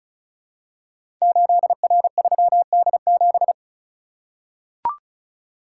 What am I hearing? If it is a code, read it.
Morse code, heard as a keyed tone, about 36 wpm, 700 Hz. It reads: 8R3D7
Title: Code Groups - 5 Letter:Numbers - 35wpm